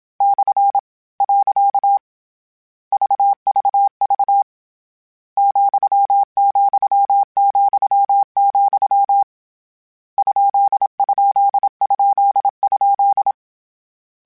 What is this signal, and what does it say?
Morse code, heard as a keyed tone, about 27 wpm, 800 Hz. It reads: / . VVV , , , , ? ? ? ?
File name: Misc / Misc-mp3s / Punct @25